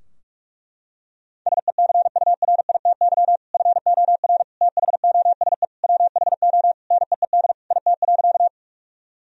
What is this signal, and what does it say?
Morse code, heard as a keyed tone, about 42 wpm, 700 Hz. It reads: SECURITY FOR THOSE WHO NEED IT.
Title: Sentences from Top 1000 Words - Review Part 4 - 45wpm